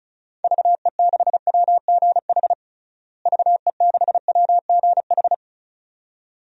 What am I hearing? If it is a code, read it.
Morse code, heard as a keyed tone, about 34 wpm, 700 Hz. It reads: VE6WGH VE6WGH
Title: DXCC-100 Call Signs - 6 Characters - Repeated 2x - 35wpm